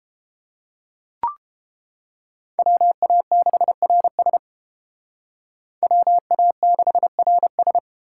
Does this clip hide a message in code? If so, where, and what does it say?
Yes: Morse code from 2.6 to 7.8 s, heard as a keyed tone, about 30 wpm, 700 Hz. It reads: WA6RS WA6RS